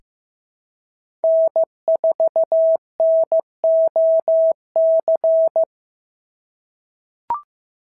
Morse code, heard as a keyed tone, about 15 wpm, 650 Hz. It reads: N4NOC
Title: SOTA QSO Practice 15wpm